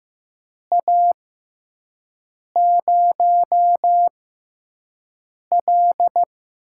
Morse code, heard as a keyed tone, about 15 wpm, spaced as for 5 wpm, 700 Hz. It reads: A0L
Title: USA Call Signs - 6x Character Spacing - 15wpm